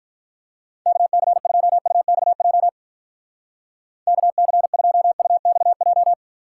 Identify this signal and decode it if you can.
Morse code, heard as a keyed tone, about 47 wpm, 700 Hz. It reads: KC2UXJ KC2UXJ